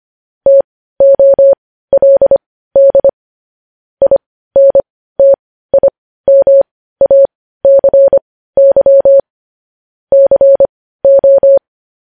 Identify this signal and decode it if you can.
Morse code, heard as a keyed tone, about 25 wpm, spaced as for 15 wpm, 550 Hz. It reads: TOLD INTIMACY CO
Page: news headlines in morse code at 15 wpm